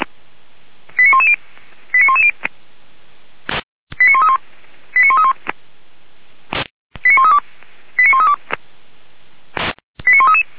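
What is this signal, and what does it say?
Five-tone selective call (ZVEI), heard as quick runs of five tones, tones of about 70 ms. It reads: 89110 89110 89131 89131 89132 89132 89133
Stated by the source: Five-tone ZVEI call received on an SDR.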